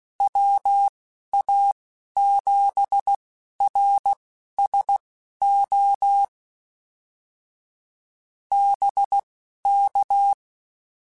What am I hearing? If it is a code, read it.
Morse code, heard as a keyed tone, about 16 wpm, spaced as for 11 wpm, 800 Hz. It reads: WA7RSO BK
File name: QSOs / QSOs-mp3s / WD0BGP 1@13